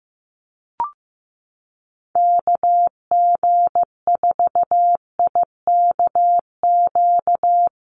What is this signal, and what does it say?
Morse code, heard as a keyed tone, about 14 wpm, 700 Hz. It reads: KG4IKQ
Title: Call Signs Repeated 2x - 15wpm